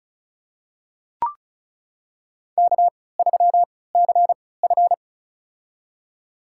Morse code, heard as a keyed tone, about 36 wpm, spaced as for 20 wpm, 700 Hz. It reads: K3CF
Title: USA Call Signs - 4 Characters - 3x Character Spacing - 35wpm